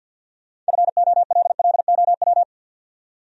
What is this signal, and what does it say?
Morse code, heard as a keyed tone, about 62 wpm, 700 Hz. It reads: FOLLOW
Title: Speed-Racing Top 300 Words 50wpm